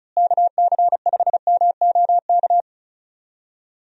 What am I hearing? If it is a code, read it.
Morse code, heard as a keyed tone, about 34 wpm, 700 Hz. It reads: KC5MOK
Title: USA Call Signs - 6 Characters - Repeated 3x - 35wpm